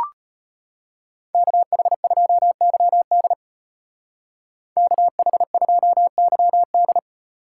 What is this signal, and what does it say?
Morse code, heard as a keyed tone, about 34 wpm, 700 Hz. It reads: KH2YD KH2YD